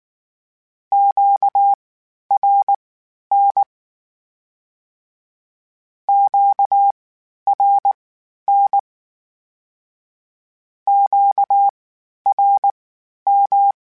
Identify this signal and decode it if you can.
Morse code, heard as a keyed tone, about 19 wpm, spaced as for 11 wpm, 800 Hz. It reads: QRN QRN QRM
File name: (spaced) / Q-Calls Spaced-mp3s / Q Calls Twice sp @18